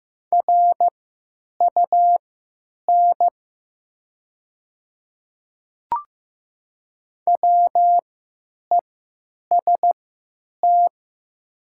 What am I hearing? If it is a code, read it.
Morse code, heard as a keyed tone, about 15 wpm, spaced as for 9 wpm, 700 Hz. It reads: RUN WEST
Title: 3x Character Spacing - Top 500 Words - 15wpm